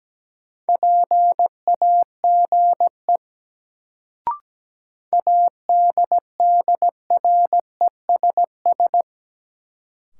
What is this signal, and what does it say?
Morse code, heard as a keyed tone, about 17 wpm, 700 Hz. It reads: PAGE ADDRESS
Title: Sets of 5 Words Encore - New Word Drill - 17wpm